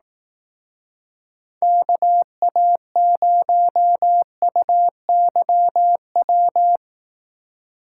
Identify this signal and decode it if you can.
Morse code, heard as a keyed tone, about 18 wpm, 700 Hz. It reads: KA0UYW
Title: Speed Racing - USA 2x3 Call Signs 15wpm